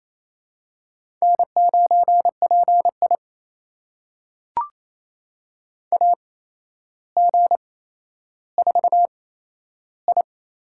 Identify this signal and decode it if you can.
Morse code, heard as a keyed tone, about 28 wpm, spaced as for 12 wpm, 700 Hz. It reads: N9PI AG4I
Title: USA Call Signs - 4 Characters - 8x Character Spacing - 28wpm